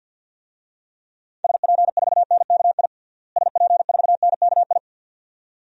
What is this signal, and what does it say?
Morse code, heard as a keyed tone, about 47 wpm, 700 Hz. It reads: SP4NKI SP4NKI
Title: DXCC-100 Call Signs - 6 Characters - Repeated 2x - 50wpm